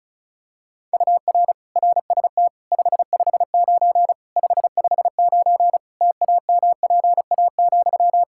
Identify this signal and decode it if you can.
Morse code, heard as a keyed tone, about 34 wpm, 700 Hz. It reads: UR RST 559 559 TAMPA,
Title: Practice QSO - Exchange 1 - Beginning - 35wpm